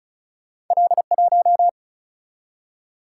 Morse code, heard as a keyed tone, about 36 wpm, 700 Hz. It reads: L1
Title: Single-Letter Single-Number 35wpm